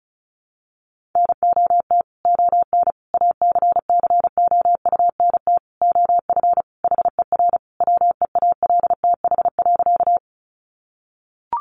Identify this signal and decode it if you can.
Morse code, heard as a keyed tone, about 34 wpm, 700 Hz. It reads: NOT ON ACCOUNT OF HER WEALTH.
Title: Sentences from Top 1200 Words 35wpm